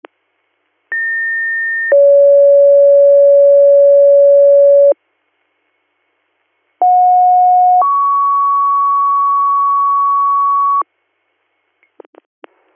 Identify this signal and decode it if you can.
Two-tone page, heard as a one-second tone, then a three-second tone, about 1800 then 560 Hz, about 730 then 1120 Hz. It is heard from 0.9 to 10.8 s.